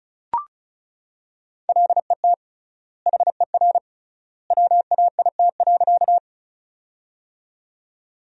Morse code, heard as a keyed tone, about 36 wpm, 700 Hz. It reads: LET HER WAIT.
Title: 3x Word-Spacing - Sentences from Top 500 Words - 35wpm